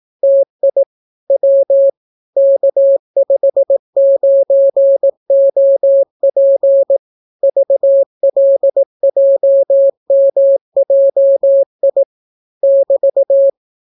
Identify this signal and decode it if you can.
Morse code, heard as a keyed tone, about 18 wpm, 550 Hz. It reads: TI W K59OP VLJMJI <BT>